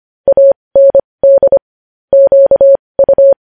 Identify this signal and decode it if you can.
Morse code, heard as a keyed tone, about 25 wpm, spaced as for 20 wpm, 550 Hz. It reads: AND QU
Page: news-headlines-in-morse-code-at-20-wpm